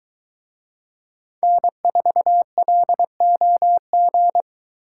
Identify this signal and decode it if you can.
Morse code, heard as a keyed tone, about 22 wpm, 700 Hz. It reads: N4LOG